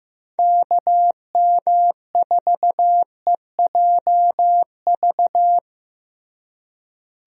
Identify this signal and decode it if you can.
Morse code, heard as a keyed tone, about 15 wpm, 700 Hz. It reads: KM4EJV
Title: USA Call Signs - 6 Characters - Repeated 2x - 15wpm